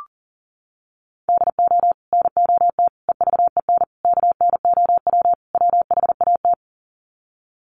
Morse code, heard as a keyed tone, about 40 wpm, 700 Hz. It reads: DO NOT EVEN KNOW WHAT
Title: Sets of 5 Words Encore 40wpm